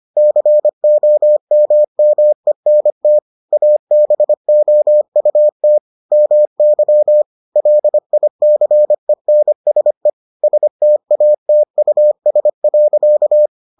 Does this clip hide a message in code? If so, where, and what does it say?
Yes: Morse code from 0.2 to 13.5 s, heard as a keyed tone, about 25 wpm, 600 Hz. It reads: COMMENT ABOUT MY LICENSE STATUS.